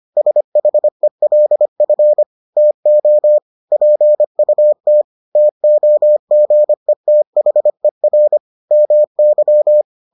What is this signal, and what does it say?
Morse code, heard as a keyed tone, about 25 wpm, 600 Hz. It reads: SHELF TO PUT TOGETHER MY